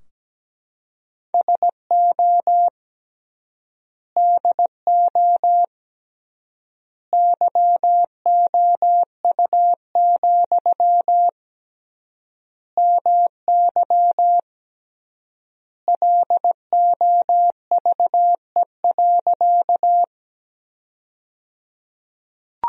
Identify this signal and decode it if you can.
Morse code, heard as a keyed tone, about 17 wpm, 700 Hz. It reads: SO DO YOU, MY LOVE.
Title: 3x Word-Spacing - Sentences from Top 400 Words - 17wpm